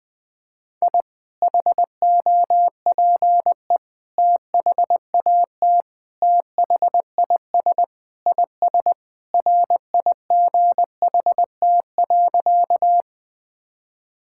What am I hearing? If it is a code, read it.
Morse code, heard as a keyed tone, about 20 wpm, 700 Hz. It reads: I HOPE THAT THIS IS RIGHT.
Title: Sentences with 3-Word Phrases - 20wpm